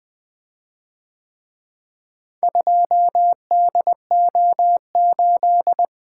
Morse code, heard as a keyed tone, about 24 wpm, 700 Hz. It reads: 2DO8
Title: Speed-Racing - Four Letter-Number Combinations - 20wpm